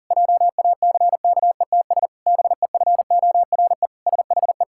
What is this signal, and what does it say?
Morse code, heard as a keyed tone, about 37 wpm, 700 Hz. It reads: JACKETS BEFORE SHE
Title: Pangram Challenge v2 - 40wpm